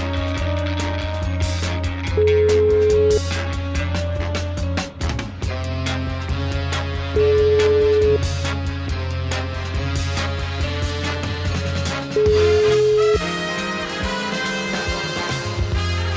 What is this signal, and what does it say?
Ringback tone (Europe, 425 Hz), with noise.